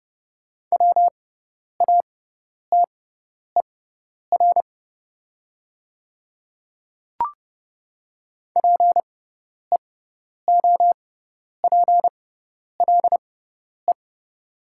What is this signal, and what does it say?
Morse code, heard as a keyed tone, about 30 wpm, spaced as for 10 wpm, 700 Hz. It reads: WATER PEOPLE
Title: 6x Character Spacing - Top 200 Words - 30wpm